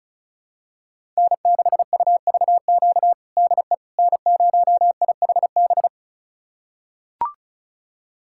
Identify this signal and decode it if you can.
Morse code, heard as a keyed tone, about 36 wpm, 700 Hz. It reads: N6UVQ DE N0IHB
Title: USA Callsign DE Callsign - 5 Characters - 35wpm